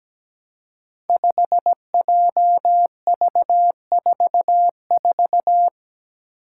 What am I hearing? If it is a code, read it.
Morse code, heard as a keyed tone, about 17 wpm, 700 Hz. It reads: 5JV44
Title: Five Challenging Letter-Number Combinations 17wpm